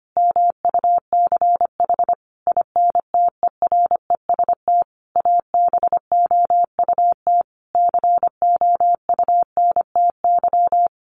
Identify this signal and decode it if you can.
Morse code, heard as a keyed tone, about 25 wpm, 700 Hz. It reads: MUCH INTEREST ABOUT COUNTY
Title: Sentences from Top 1200 Words 25wpm